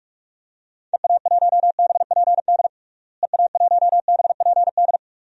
Morse code, heard as a keyed tone, about 47 wpm, 700 Hz. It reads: EA1BPD EA1BPD
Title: DXCC-100 Call Signs - 6 Characters - Repeated 2x - 45wpm